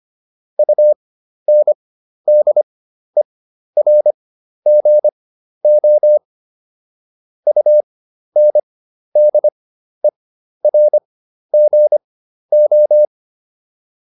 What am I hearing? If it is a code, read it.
Morse code, heard as a keyed tone, about 25 wpm, spaced as for 12 wpm, 600 Hz. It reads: UNDERGO UNDERGO